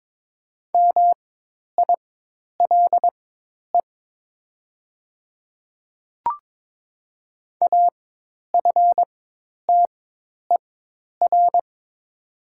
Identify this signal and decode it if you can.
Morse code, heard as a keyed tone, about 22 wpm, spaced as for 10 wpm, 700 Hz. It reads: MILE AFTER
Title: 4x Character Spacing - Top 500 Words - 22wpm